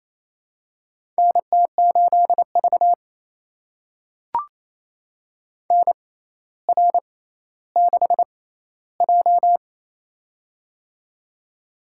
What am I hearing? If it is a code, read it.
Morse code, heard as a keyed tone, about 28 wpm, spaced as for 14 wpm, 700 Hz. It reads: NT8V NR6J
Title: USA Call Signs - 4 Characters - 6x Character Spacing - 28wpm